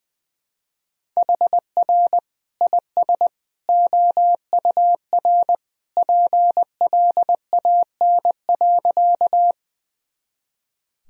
Morse code, heard as a keyed tone, about 21 wpm, 700 Hz. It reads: HR IS OUR PLAN.
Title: Sentences from Top 500 Words and Abbreviations 20wpm